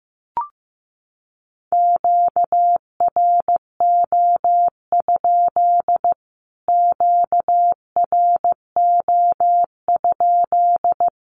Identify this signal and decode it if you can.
Morse code, heard as a keyed tone, about 15 wpm, 700 Hz. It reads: QRO? QRO?
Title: Q-Codes Repeated 3x 15wpm